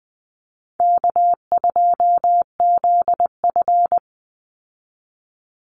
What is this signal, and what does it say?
Morse code, heard as a keyed tone, about 20 wpm, 700 Hz. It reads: K2ZF
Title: Call Signs - 20wpm